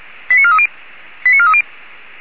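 Five-tone selective call (ZVEI), heard as quick runs of five tones, tones of about 70 ms. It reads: 89430 89430
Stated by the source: ZVEI five-tone selective call received on an SDR.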